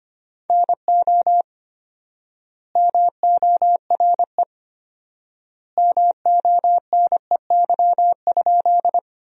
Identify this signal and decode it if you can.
Morse code, heard as a keyed tone, about 25 wpm, 700 Hz. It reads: NO MORE MONEY?
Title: 4x Word-Spacing - Sentences from Top 400 Words - 25wpm